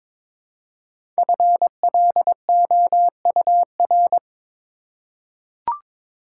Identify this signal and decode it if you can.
Morse code, heard as a keyed tone, about 22 wpm, 700 Hz. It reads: FLOUR
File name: 5 Letter Words - First 3 Letters Given - 22wpm